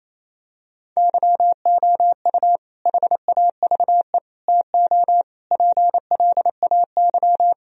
Morse code, heard as a keyed tone, about 28 wpm, 700 Hz. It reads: YOU HAVE TO PLAY
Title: Sentences from Top 1000 Words - Review Part 1 - 28wpm